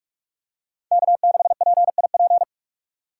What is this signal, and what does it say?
Morse code, heard as a keyed tone, about 45 wpm, 700 Hz. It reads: K6PIP